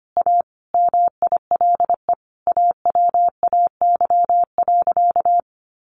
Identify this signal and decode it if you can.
Morse code, heard as a keyed tone, about 25 wpm, 700 Hz. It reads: A MILE AWAY.
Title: Sentences from Top 1300 Words 25wpm